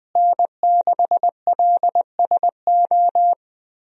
Morse code, heard as a keyed tone, about 21 wpm, 700 Hz. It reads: N6LSO
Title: USA Call Signs - 5 Characters - Repeated 2x - 20wpm